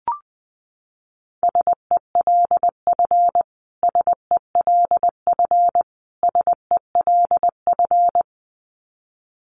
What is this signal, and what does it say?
Morse code, heard as a keyed tone, about 20 wpm, 700 Hz. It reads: SELF SELF SELF
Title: Top 500 Words Repeated 20wpm